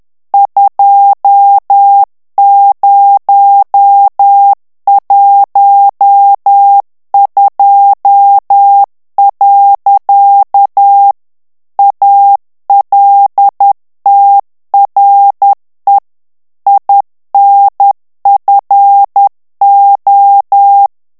Morse code, heard as a keyed tone, about 10 wpm, 800 Hz. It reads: 2012. ALTRE INFO
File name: morse mp3 / ITALIEN texte clair 1 vitesse 10 mots minute